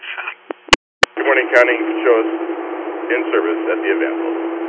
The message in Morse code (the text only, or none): none